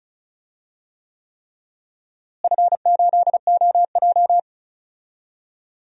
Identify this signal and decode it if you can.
Morse code, heard as a keyed tone, about 36 wpm, 700 Hz. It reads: F8OJ
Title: DXCC-100 Call Signs - 4 Characters - Repeated 3x - 35wpm